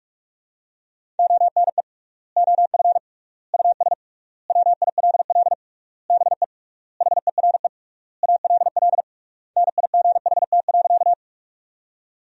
Morse code, heard as a keyed tone, about 46 wpm, 700 Hz. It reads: ONE OF US WILL BE HERE ALL NIGHT.
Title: 3x Word-Spacing - Sentences from Top 300 Words - 45wpm